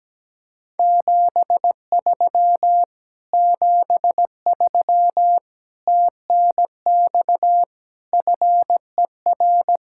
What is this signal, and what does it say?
Morse code, heard as a keyed tone, about 17 wpm, 700 Hz. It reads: 73 73 TNX FER
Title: Speed-Racing Common QSO Phrases - 17wpm